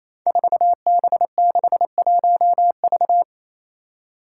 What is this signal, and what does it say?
Morse code, heard as a keyed tone, about 28 wpm, 700 Hz. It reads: 4B61V
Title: Five Challenging Letter-Number Combinations 28wpm